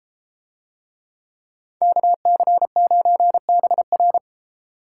Morse code, heard as a keyed tone, about 30 wpm, 700 Hz. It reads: KC9BR